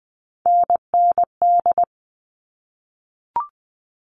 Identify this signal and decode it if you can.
Morse code, heard as a keyed tone, about 20 wpm, 700 Hz. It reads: NND
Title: All Cut Numbers - 3-Digit 20wpm